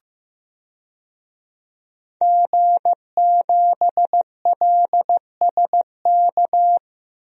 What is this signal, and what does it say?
Morse code, heard as a keyed tone, about 15 wpm, 700 Hz. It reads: G7LSK
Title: DXCC-100 Call Signs - 5 Characters - Repeated 2x - 15wpm